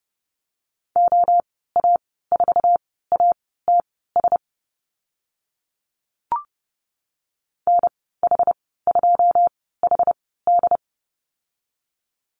Morse code, heard as a keyed tone, about 30 wpm, spaced as for 17 wpm, 700 Hz. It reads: OA4ATS NH2HD
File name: Lesson 14g - Number 2 - Call Signs 30wpm